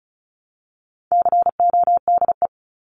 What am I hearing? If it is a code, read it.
Morse code, heard as a keyed tone, about 34 wpm, 700 Hz. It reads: CODE
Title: Top 1501-1750 Words 35wpm